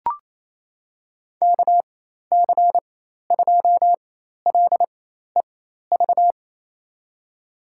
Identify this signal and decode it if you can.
Morse code, heard as a keyed tone, about 28 wpm, spaced as for 13 wpm, 700 Hz. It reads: KC2LEV
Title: USA Call Signs - 4x Character Spacing - 28wpm